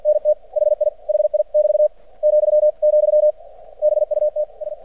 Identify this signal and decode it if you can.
Morse code, heard as a keyed tone, about 50 wpm, 600 Hz. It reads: NT HI HI <BT> , , BUT S
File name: archived / CRSnet